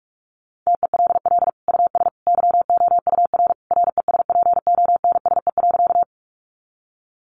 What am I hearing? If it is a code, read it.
Morse code, heard as a keyed tone, about 44 wpm, 700 Hz. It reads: TELL US YOUR RESPONSE.
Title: Sentences from Top 1400 Words 45wpm